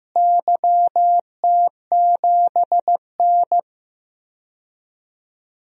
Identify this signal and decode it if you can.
Morse code, heard as a keyed tone, about 15 wpm, 700 Hz. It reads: YT7N
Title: DXCC-100 Call Signs - 4 Characters - Repeated 3x - 15wpm